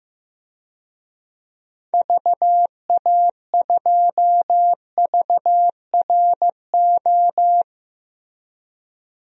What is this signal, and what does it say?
Morse code, heard as a keyed tone, about 15 wpm, 700 Hz. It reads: VA2VRO
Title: DXCC-100 Call Signs - 6 Characters - Repeated 2x - 15wpm